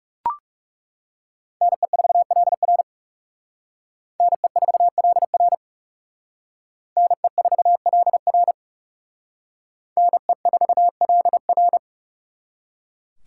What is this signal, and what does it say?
Morse code, heard as a keyed tone, about 38 wpm, 700 Hz. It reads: NE4LR NE4LR NE4LR NE4LR